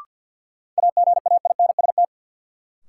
Morse code, heard as a keyed tone, about 53 wpm, 700 Hz. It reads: AGAINST
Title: Sets of 2 Words - New Word Drill - 50wpm